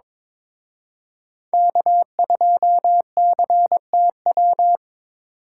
Morse code, heard as a keyed tone, about 26 wpm, 700 Hz. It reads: K2CTW
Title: Speed Racing - USA 1x3 Call Signs 22wpm